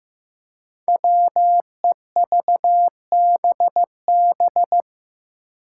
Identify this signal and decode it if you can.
Morse code, heard as a keyed tone, about 15 wpm, 700 Hz. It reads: WEVBB
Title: Five Challenging Letter-Number Combinations 15wpm